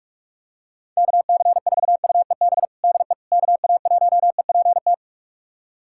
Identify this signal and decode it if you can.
Morse code, heard as a keyed tone, about 46 wpm, 700 Hz. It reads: KC4UEB DE KA1EPT
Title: USA Callsign DE Callsign - 6 Characters - 45wpm